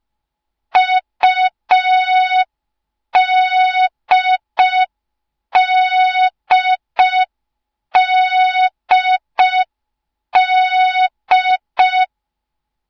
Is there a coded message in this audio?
Yes: Morse code from 0.8 to 12.1 s, heard as a keyed tone, about 5 wpm, 3700 Hz. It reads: UDDDD